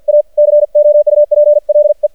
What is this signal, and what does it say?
Morse code, heard as a keyed tone, about 50 wpm, 590 Hz. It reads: A COUPLE
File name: archived / CRSnet